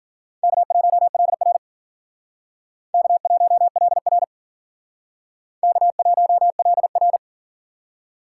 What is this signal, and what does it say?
Morse code, heard as a keyed tone, about 50 wpm, 700 Hz. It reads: K1LR K1LR K1LR
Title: Speed Racing - USA Call Signs 40wpm